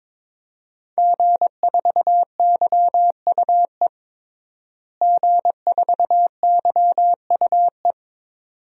Morse code, heard as a keyed tone, about 22 wpm, 700 Hz. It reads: G4YUE G4YUE
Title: DXCC-100 Call Signs - 5 Characters - Repeated 2x - 22wpm